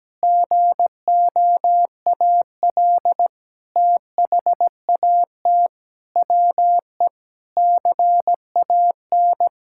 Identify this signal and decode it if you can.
Morse code, heard as a keyed tone, about 17 wpm, 700 Hz. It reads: GOAL THAT WE CAN
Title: Sentences from Top 1000 Words - Review Part 6 - 17wpm